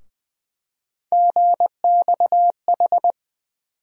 Morse code, heard as a keyed tone, about 20 wpm, 700 Hz. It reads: GXH
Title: USA Call Sign Postfixes - 3 Letters 20wpm